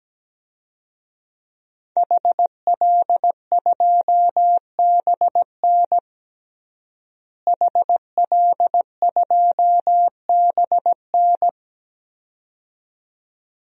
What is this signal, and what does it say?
Morse code, heard as a keyed tone, about 17 wpm, 700 Hz. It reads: HL2BN HL2BN
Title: DXCC-100 Call Signs - 5 Characters - Repeated 2x - 17wpm